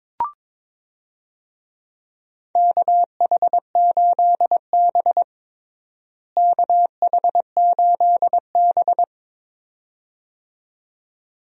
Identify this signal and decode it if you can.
Morse code, heard as a keyed tone, about 22 wpm, 700 Hz. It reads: KH8B KH8B
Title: USA Call Signs - 4 Characters - Repeated 2x - 22wpm